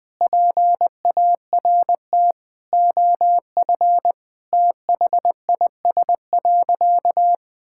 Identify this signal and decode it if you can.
Morse code, heard as a keyed tone, about 20 wpm, 700 Hz. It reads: PART OF THIS.
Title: Sentences with 5-Word Phrases - 20wpm